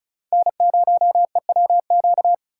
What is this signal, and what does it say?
Morse code, heard as a keyed tone, about 34 wpm, 700 Hz. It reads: N0EWQ
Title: USA Call Signs - 5 Characters - 2x Character Spacing - 35wpm